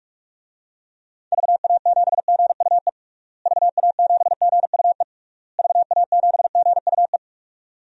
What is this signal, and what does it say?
Morse code, heard as a keyed tone, about 41 wpm, 700 Hz. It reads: VA7GUE VA7GUE VA7GUE
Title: DXCC-100 Call Signs - 6 Characters - Repeated 3x - 45wpm